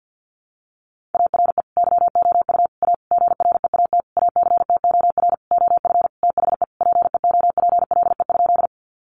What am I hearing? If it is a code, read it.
Morse code, heard as a keyed tone, about 50 wpm, 700 Hz. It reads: ARE YOU A GREAT ACTOR OF THE PEOPLE?
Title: Sentences from Top 1500 Words 50wpm